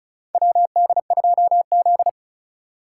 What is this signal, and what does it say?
Morse code, heard as a keyed tone, about 34 wpm, 700 Hz. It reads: WD2Z
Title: USA Call Signs - 4x Character Spacing - 35wpm